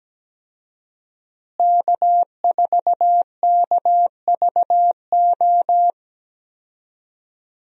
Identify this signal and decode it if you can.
Morse code, heard as a keyed tone, about 17 wpm, 700 Hz. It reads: K4KVO
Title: USA Call Signs - 5 Characters - Repeated 3x - 17wpm